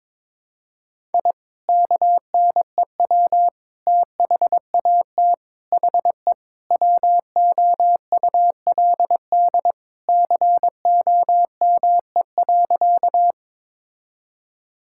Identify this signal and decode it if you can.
Morse code, heard as a keyed tone, about 22 wpm, 700 Hz. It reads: I KNEW THAT HE WOULD COME.
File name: practice / 22wpm / Sentences with 2-Word Phrases - 22wpm